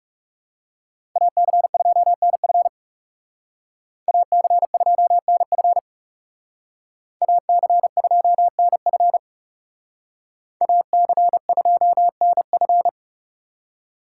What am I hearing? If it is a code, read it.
Morse code, heard as a keyed tone, about 36 wpm, 700 Hz. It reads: AC2NF AC2NF AC2NF AC2NF
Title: Speed Racing - USA 2x2 Call Signs 30wpm